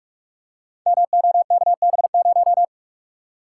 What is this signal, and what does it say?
Morse code, heard as a keyed tone, about 44 wpm, 700 Hz. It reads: MOKB0
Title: Code Groups - 5 Letter:Numbers - 45wpm